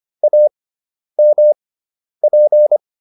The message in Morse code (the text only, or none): AMP